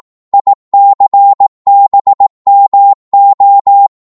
Morse code, heard as a keyed tone, about 18 wpm, 800 Hz. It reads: ICBMO
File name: Morse / ltr-18wpm